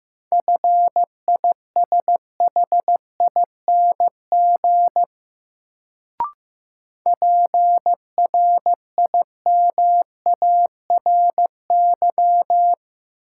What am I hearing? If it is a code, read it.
Morse code, heard as a keyed tone, about 15 wpm, 700 Hz. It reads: FISHING PRIMARY
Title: Sentences from Top 1000 Words - Review Part 5 - New Word Drill - 15wpm